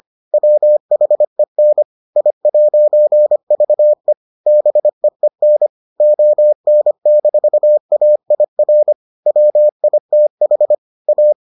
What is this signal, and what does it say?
Morse code, heard as a keyed tone, about 25 wpm, 600 Hz. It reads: WHEN I'VE BEEN ON-AIR WITH A